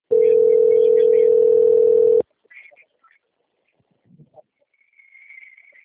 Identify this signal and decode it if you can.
Ringback tone (North America), 440 + 480 Hz.
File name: out- - - - - .2379